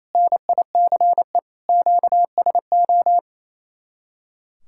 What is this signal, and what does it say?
Morse code, heard as a keyed tone, about 36 wpm, 700 Hz. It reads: NICE QSO